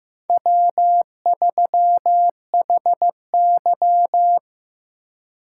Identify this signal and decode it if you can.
Morse code, heard as a keyed tone, about 16 wpm, 700 Hz. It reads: W3HY
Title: USA Call Signs - 4x Character Spacing - 15wpm